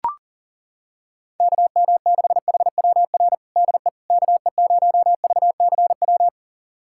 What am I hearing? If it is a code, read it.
Morse code, heard as a keyed tone, about 40 wpm, 700 Hz. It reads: KM6HWR DE KE0VCW
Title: USA Callsign DE Callsign - 6 Characters - 40wpm